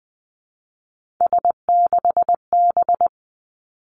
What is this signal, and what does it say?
Morse code, heard as a keyed tone, about 20 wpm, 700 Hz. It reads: S6B
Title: Three Challenging Letter-Number Combinations 20wpm